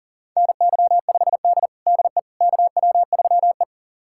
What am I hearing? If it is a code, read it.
Morse code, heard as a keyed tone, about 40 wpm, 700 Hz. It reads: NY5D DE KW3E
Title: USA Callsign DE Callsign - 4 Characters - 40wpm